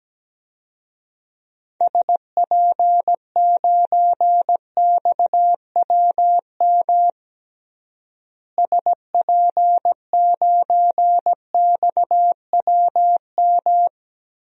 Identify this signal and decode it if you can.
Morse code, heard as a keyed tone, about 17 wpm, 700 Hz. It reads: SP9XWM SP9XWM
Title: DXCC-100 Call Signs - 6 Characters - Repeated 3x - 17wpm